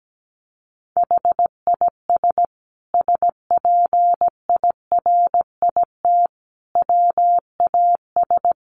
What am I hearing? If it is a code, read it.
Morse code, heard as a keyed tone, about 17 wpm, 700 Hz. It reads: HIS SPIRIT WAS
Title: Sentences from Top 1400 Words 17wpm